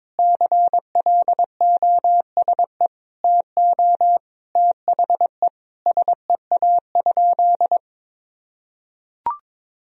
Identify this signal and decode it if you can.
Morse code, heard as a keyed tone, about 22 wpm, 700 Hz. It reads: CLOSE TO THE SEA?
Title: Sentences from Top 300 Words and Abbreviations 22wpm